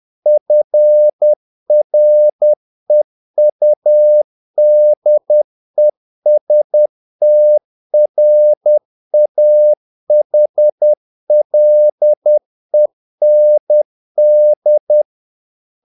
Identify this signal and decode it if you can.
Morse code, heard as a keyed tone, about 10 wpm, 600 Hz. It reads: FREUDESTRAHLEND
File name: dk5ke / Humor-50BpM